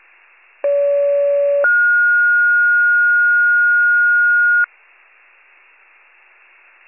Two-tone page, heard as a one-second tone, then a three-second tone, about 570 then 1430 Hz. It is heard from 0.6 to 4.7 s.